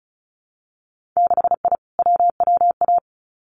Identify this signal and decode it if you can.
Morse code, heard as a keyed tone, about 36 wpm, 700 Hz. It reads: 6I WWA
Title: Winter Field Day Exchanges Encore 35wpm